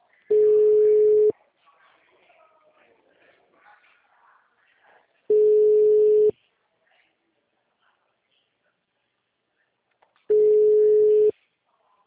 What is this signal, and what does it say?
Ringback tone (Europe, 425 Hz).
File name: out- -2019- - - .101